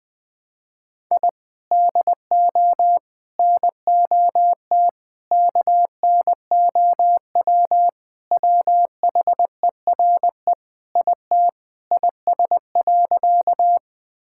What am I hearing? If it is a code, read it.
Morse code, heard as a keyed tone, about 20 wpm, 700 Hz. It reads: I DO NOT KNOW WHERE IT IS.
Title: Sentences with 5-Word Phrases - 20wpm